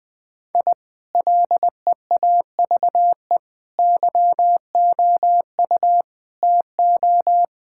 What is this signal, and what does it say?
Morse code, heard as a keyed tone, about 20 wpm, 700 Hz. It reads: I LEAVE YOU TO
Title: Sentences from Top 1000 Words - Review Part 5 - 20wpm